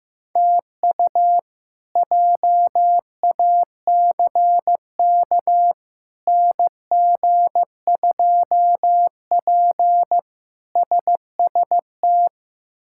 Morse code, heard as a keyed tone, about 15 wpm, 700 Hz. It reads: TU JACK NG2P SST
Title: K1USN SST QSO Practice - 15wpm